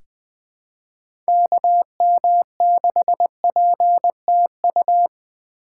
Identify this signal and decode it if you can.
Morse code, heard as a keyed tone, about 21 wpm, 700 Hz. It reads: KM6PTU
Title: USA Call Signs - 6 Characters - 2x Character Spacing - 20wpm